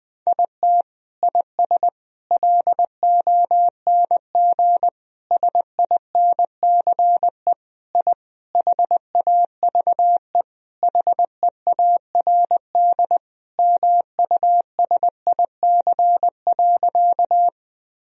Morse code, heard as a keyed tone, about 21 wpm, 700 Hz. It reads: IT IS LONG SINCE I HAVE HEARD MUSIC.
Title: Sentences from Top 1000 Words - Review Part 1 - 20wpm